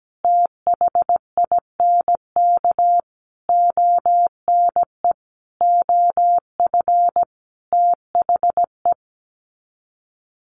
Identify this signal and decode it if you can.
Morse code, heard as a keyed tone, about 17 wpm, 700 Hz. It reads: THINK ONE OF THE
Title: Sets of 5 Words 17wpm